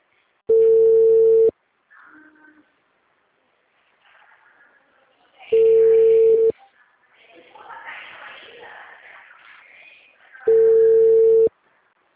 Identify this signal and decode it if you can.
Ringback tone (China, 450 Hz).
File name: out- -2045- - - .44465